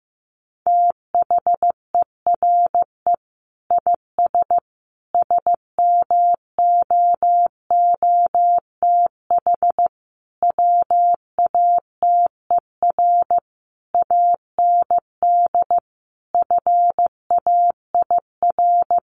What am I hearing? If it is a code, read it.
Morse code, heard as a keyed tone, about 14 wpm, 700 Hz. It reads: THERE IS SMOOTH WATER AND FAIR
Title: Sentences from Top 1400 Words 15wpm